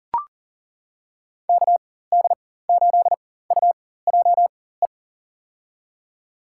Morse code, heard as a keyed tone, about 39 wpm, spaced as for 18 wpm, 700 Hz. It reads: KD8UJE
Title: USA Call Signs - 4x Character Spacing - 40wpm